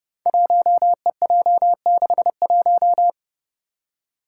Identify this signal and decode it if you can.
Morse code, heard as a keyed tone, about 30 wpm, 700 Hz. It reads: 1EJ61
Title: Five Challenging Letter-Number Combinations 30wpm